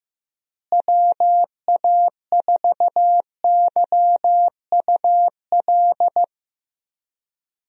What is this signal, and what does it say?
Morse code, heard as a keyed tone, about 15 wpm, 700 Hz. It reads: WA4YUL
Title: USA Call Signs - 6 Characters - Repeated 3x - 15wpm